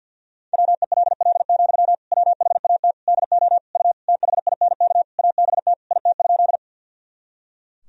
Morse code, heard as a keyed tone, about 51 wpm, 700 Hz. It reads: WELL, WHAT DO U THINK ABT IT?
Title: Sentences from Top 200 Words and Abbreviations 50wpm